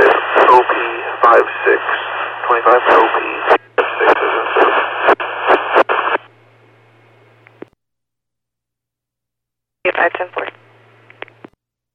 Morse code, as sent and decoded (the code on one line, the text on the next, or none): none